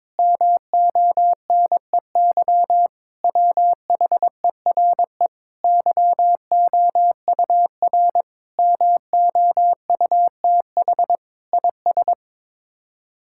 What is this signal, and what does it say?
Morse code, heard as a keyed tone, about 23 wpm, 700 Hz. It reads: MONEY WHERE YOUR MOUTH IS
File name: practice / 22wpm / Sets of 7 Words - 22wpm